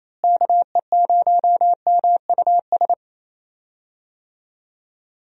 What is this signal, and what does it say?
Morse code, heard as a keyed tone, about 28 wpm, 700 Hz. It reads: KE0MUS
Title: USA Call Signs - 6 Characters - Repeated 3x - 28wpm